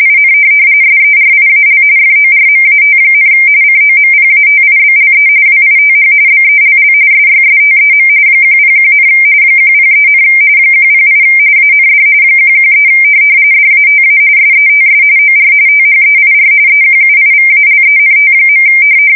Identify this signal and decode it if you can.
RTTY, about 45.45 baud. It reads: IN SERIAL FORMAT, COMPATIBLE WITH BELL SYSTEM 101, 103, 108 AND 113 DATA SETS OR THEIR EQUIVALENT.
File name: fsk170 / teletype-corp